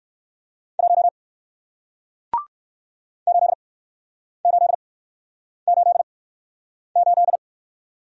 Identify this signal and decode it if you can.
Morse code, heard as a keyed tone, about 60 wpm, 700 Hz. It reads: 1 8 8 8 8